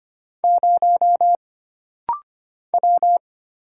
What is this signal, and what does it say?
Morse code, heard as a keyed tone, about 25 wpm, 700 Hz. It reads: 0 W
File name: Single Character (Letter:Number:Punctuation:Prosign) - Rapid-Fire - 25wpm